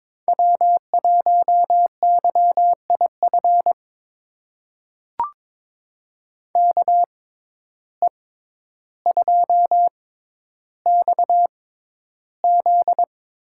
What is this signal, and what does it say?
Morse code, heard as a keyed tone, about 22 wpm, spaced as for 11 wpm, 700 Hz. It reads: W1YIF KE2XZ
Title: USA Call Signs - 6x Character Spacing - 22wpm